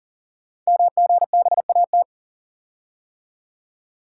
Morse code, heard as a keyed tone, about 41 wpm, 690 Hz. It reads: MGBAT
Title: Code Groups - 5 Letter:Numbers - 40wpm